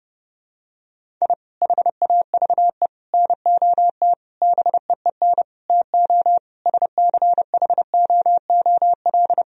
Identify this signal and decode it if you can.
Morse code, heard as a keyed tone, about 32 wpm, 700 Hz. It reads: I HAVE NOT BEEN TO SCHOOL
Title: Sentences from Top 1000 Words - Review Part 1 - 30wpm